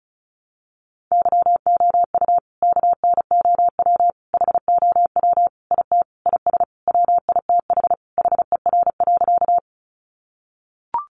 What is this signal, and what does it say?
Morse code, heard as a keyed tone, about 34 wpm, 700 Hz. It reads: YOU KNOW HOW IT IS WITH HER.
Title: Sentences from Top 100 Words 35wpm